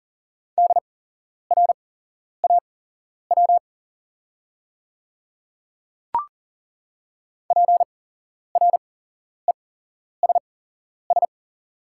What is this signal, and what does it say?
Morse code, heard as a keyed tone, about 40 wpm, spaced as for 11 wpm, 700 Hz. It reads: DRAW PRESS
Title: 8x Character Spacing - Top 500 Words - 40wpm